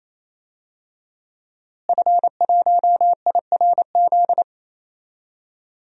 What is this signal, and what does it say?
Morse code, heard as a keyed tone, about 28 wpm, 700 Hz. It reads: F1IRZ
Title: DXCC-100 Call Signs - 5 Characters - Repeated 3x - 28wpm